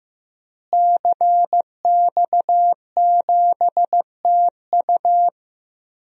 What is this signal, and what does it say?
Morse code, heard as a keyed tone, about 15 wpm, 700 Hz. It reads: CX7TU
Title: DXCC-100 Call Signs - 5 Characters - Repeated 3x - 15wpm